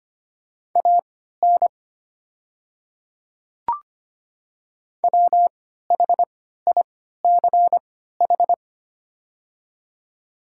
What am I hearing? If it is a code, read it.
Morse code, heard as a keyed tone, about 25 wpm, spaced as for 14 wpm, 700 Hz. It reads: AN WHICH
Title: 3x Character Spacing - Top 100 Words - 25wpm